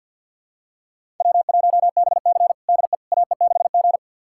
Morse code, heard as a keyed tone, about 49 wpm, 700 Hz. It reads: W1BC DE AE6G
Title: USA Callsign DE Callsign - 4 Characters - 50wpm